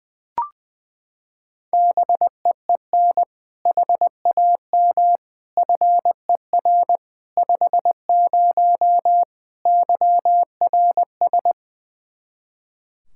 Morse code, heard as a keyed tone, about 20 wpm, 700 Hz. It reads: BEEN HAM FER 50 YRS
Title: Practice QSO - Exchange 3 - More Info - 20wpm